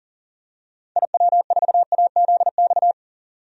Morse code, heard as a keyed tone, about 40 wpm, 700 Hz. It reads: IW4AZX
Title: DXCC-100 Call Signs - 6 Characters - 40wpm